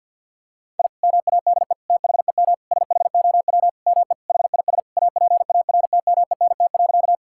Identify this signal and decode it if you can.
Morse code, heard as a keyed tone, about 50 wpm, 700 Hz. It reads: I MADE THEM SHOW ME HIS APARTMENT.
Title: Sentences from Top 1000 Words - Review Part 6 - 50wpm